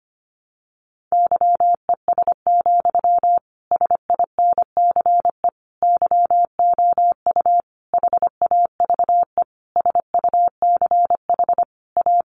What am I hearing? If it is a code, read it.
Morse code, heard as a keyed tone, about 25 wpm, 700 Hz. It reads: YES, SINCE YOU HAVE SUCH A
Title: Sentences from Top 1300 Words 25wpm